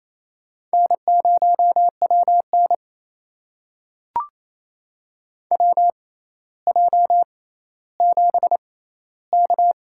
Morse code, heard as a keyed tone, about 28 wpm, spaced as for 14 wpm, 700 Hz. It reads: N0WN WJ7K
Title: USA Call Signs - 4 Characters - 6x Character Spacing - 28wpm